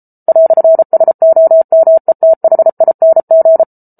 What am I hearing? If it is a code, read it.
Morse code, heard as a keyed tone, about 34 wpm, 650 Hz. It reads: "SOMETHING